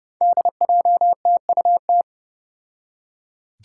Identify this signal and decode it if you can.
Morse code, heard as a keyed tone, about 32 wpm, 700 Hz. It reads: DJTUT
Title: Five Challenging Letter-Number Combinations 30wpm